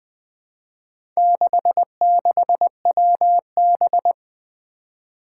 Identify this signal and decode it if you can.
Morse code, heard as a keyed tone, about 20 wpm, 700 Hz. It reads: 66WB
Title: Four Challenging Letter-Number Combinations 20wpm